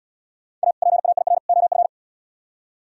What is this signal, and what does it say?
Morse code, heard as a keyed tone, about 70 wpm, 700 Hz. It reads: A PAIR OF